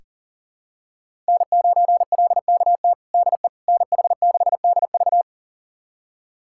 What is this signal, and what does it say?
Morse code, heard as a keyed tone, about 40 wpm, 700 Hz. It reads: N9LKT DE NH6DV
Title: USA Callsign DE Callsign - 5 Characters - 40wpm